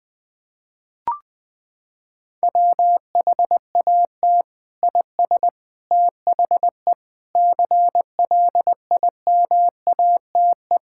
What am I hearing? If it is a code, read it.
Morse code, heard as a keyed tone, about 21 wpm, 700 Hz. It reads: WHAT IS THE CLIMATE
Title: Sentences from Top 1600 Words 20wpm